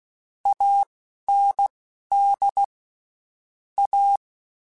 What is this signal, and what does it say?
Morse code, heard as a keyed tone, about 16 wpm, spaced as for 12 wpm, 800 Hz. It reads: AND A
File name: Spaced / Chuckles Spaced-mp3s / MiddleAge sp @15